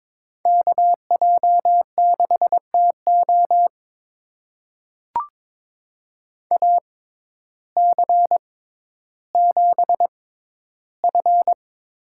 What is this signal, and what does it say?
Morse code, heard as a keyed tone, about 22 wpm, spaced as for 12 wpm, 700 Hz. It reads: KJ6TO AC7F
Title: USA Call Signs - 5 Characters - 6x Character Spacing - 22wpm